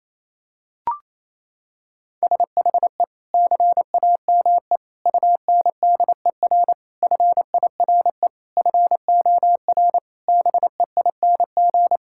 Morse code, heard as a keyed tone, about 27 wpm, 700 Hz. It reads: SHE CAME UNDER FIRE FOR BEING
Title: Sentences from Top 1000 Words - Review Part 6 - 28wpm